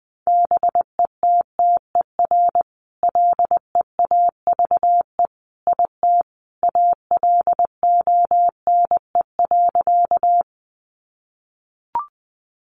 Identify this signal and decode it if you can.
Morse code, heard as a keyed tone, about 20 wpm, 700 Hz. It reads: BETTER LEAVE IT ALONE.
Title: Sentences from Top 1000 Words 20wpm